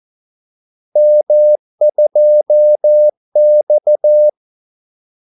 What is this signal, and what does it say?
Morse code, heard as a keyed tone, about 14 wpm, 600 Hz. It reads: M2X
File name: Callsigns-14wpm